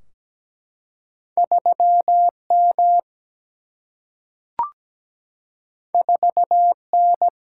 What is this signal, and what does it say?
Morse code, heard as a keyed tone, about 17 wpm, 700 Hz. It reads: 3 M 4 N
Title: Single-Letter Single-Number 17wpm